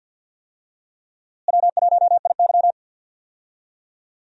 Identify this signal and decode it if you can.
Morse code, heard as a keyed tone, about 48 wpm, 700 Hz. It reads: W1IY
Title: USA Call Signs - 4 Characters - Repeated 2x - 50wpm